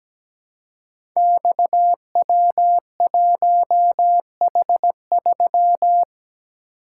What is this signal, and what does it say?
Morse code, heard as a keyed tone, about 17 wpm, 700 Hz. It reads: XW1H3
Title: Code Groups - 5 Letter:Numbers - 17wpm